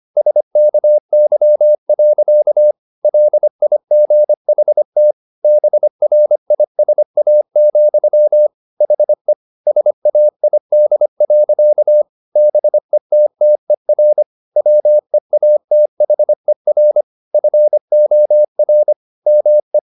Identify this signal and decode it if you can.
Morse code, heard as a keyed tone, about 25 wpm, 600 Hz. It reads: SKY. LIGHT BRISA, HE SAID. BETTER WEATHER FOR ME
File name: the old man and the sea part 1 25wpm